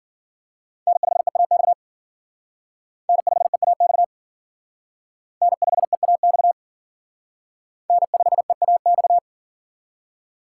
Morse code, heard as a keyed tone, about 52 wpm, 700 Hz. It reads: N5EAX N5EAX N5EAX N5EAX